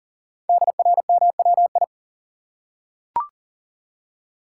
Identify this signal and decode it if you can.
Morse code, heard as a keyed tone, about 42 wpm, 700 Hz. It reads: DRMWI
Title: Code Groups - 5 Letters - 40wpm